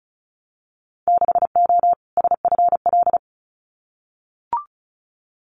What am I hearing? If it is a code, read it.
Morse code, heard as a keyed tone, about 36 wpm, 700 Hz. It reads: 6O SFL
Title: Winter Field Day Exchanges Encore 35wpm